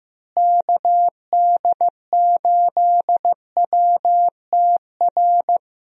Morse code, heard as a keyed tone, about 15 wpm, 700 Hz. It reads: KD8WTR
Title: USA Call Signs - 6 Characters - Repeated 3x - 15wpm